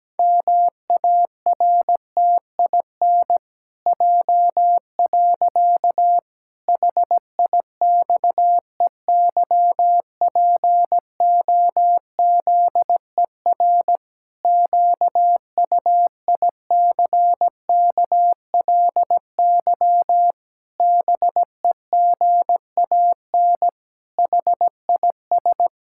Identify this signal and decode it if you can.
Morse code, heard as a keyed tone, about 17 wpm, 700 Hz. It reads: MARTIN J. HIXEYPOZER QUICKLY BEGAN HIS
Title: Pangram Challenge v2 - 17wpm